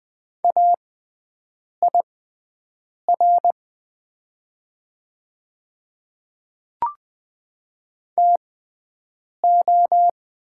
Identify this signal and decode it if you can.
Morse code, heard as a keyed tone, about 20 wpm, spaced as for 7 wpm, 700 Hz. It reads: AIR TO
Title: 6x Character Spacing - Top 200 Words - 20wpm